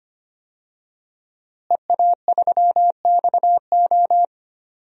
Morse code, heard as a keyed tone, about 25 wpm, 700 Hz. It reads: EA3XO
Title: DXCC-100 Call Signs - 5 Characters - Repeated 3x - 25wpm